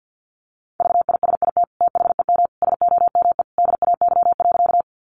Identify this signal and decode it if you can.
Morse code, heard as a keyed tone, about 50 wpm, 700 Hz. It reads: VISIT THEM SOME DAY.
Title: Sentences from Top 900 Words 50wpm